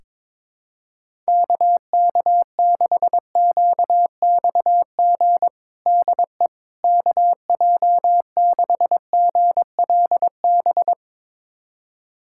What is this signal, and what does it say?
Morse code, heard as a keyed tone, about 22 wpm, 700 Hz. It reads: KK6QXG DE KJ6GLB
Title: USA Callsign DE Callsign - 6 Characters - 22wpm